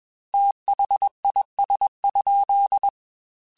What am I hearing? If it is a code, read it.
Morse code, heard as a keyed tone, about 21 wpm, 800 Hz. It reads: THIS?
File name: Technical Subjects / Technical Subjects-mp3s / OR Gate @20